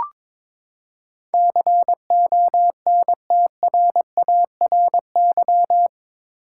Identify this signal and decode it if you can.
Morse code, heard as a keyed tone, about 21 wpm, 700 Hz. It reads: CONTRARY
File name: practice / 22wpm / Eight Letter Words - 22wpm